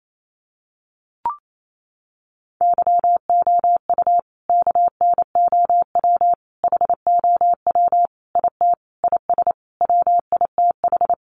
Morse code, heard as a keyed tone, about 28 wpm, 700 Hz. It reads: YOU KNOW HOW IT IS WITH
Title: Sentences from Top 100 Words 28wpm